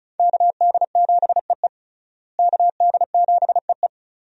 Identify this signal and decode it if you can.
Morse code, heard as a keyed tone, about 36 wpm, 700 Hz. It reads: KD7EE KD7EE
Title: USA Call Signs - 5 Characters - Repeated 2x - 35wpm